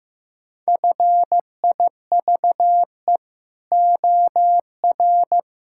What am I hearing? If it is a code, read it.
Morse code, heard as a keyed tone, about 15 wpm, 700 Hz. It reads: FIVE OR
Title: Pangram Challenge v2 - 15wpm